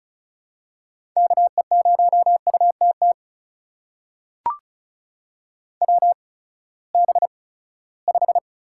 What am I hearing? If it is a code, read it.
Morse code, heard as a keyed tone, about 36 wpm, spaced as for 20 wpm, 700 Hz. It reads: KE0UTT WB5
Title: USA Call Signs - 6 Characters - 8x Character Spacing - 35wpm